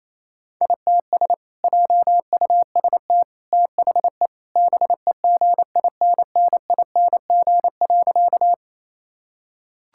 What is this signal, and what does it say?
Morse code, heard as a keyed tone, about 28 wpm, 700 Hz. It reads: ITS JUST THE BEGINNING.
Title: Sentences from Top 1000 Words - Review Part 5 - 28wpm